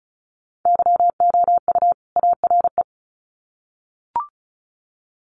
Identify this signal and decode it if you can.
Morse code, heard as a keyed tone, about 34 wpm, 700 Hz. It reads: YOU ARE
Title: Sets of 2 Words 35wpm